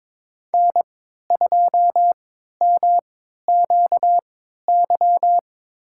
Morse code, heard as a keyed tone, about 22 wpm, spaced as for 13 wpm, 700 Hz. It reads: N2MQY
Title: USA Call Signs - 3x Character Spacing - 22wpm